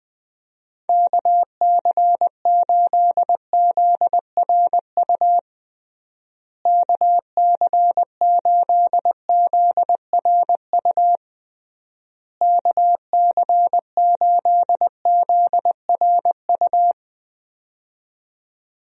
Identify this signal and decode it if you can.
Morse code, heard as a keyed tone, about 21 wpm, 700 Hz. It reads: KC8ZRU KC8ZRU KC8ZRU
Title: USA Call Signs - 6 Characters - Repeated 3x - 20wpm